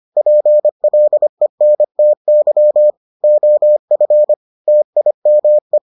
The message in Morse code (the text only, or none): PLENTY OF TIME